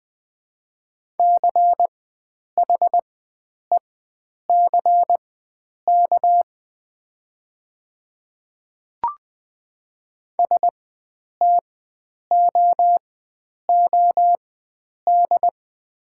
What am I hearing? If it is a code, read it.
Morse code, heard as a keyed tone, about 20 wpm, spaced as for 9 wpm, 700 Hz. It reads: CHECK STOOD
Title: 4x Character Spacing - Top 500 Words - 20wpm